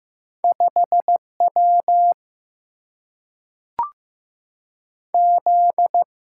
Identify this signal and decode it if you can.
Morse code, heard as a keyed tone, about 15 wpm, 700 Hz. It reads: 5 W Z
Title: Single-Letter Single-Number 15wpm